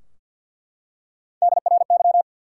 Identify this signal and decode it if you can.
Morse code, heard as a keyed tone, about 47 wpm, 700 Hz. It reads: DRY
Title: Three Letter Words - 50wpm